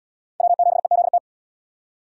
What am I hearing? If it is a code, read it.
Morse code, heard as a keyed tone, about 72 wpm, 700 Hz. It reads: W7EZT